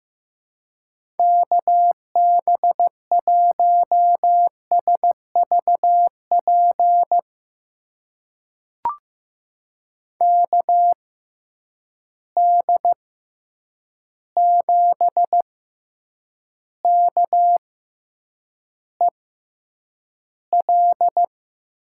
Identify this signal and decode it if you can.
Morse code, heard as a keyed tone, about 15 wpm, spaced as for 8 wpm, 700 Hz. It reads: KB1SVP KD7KEL
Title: USA Call Signs - 6x Character Spacing - 15wpm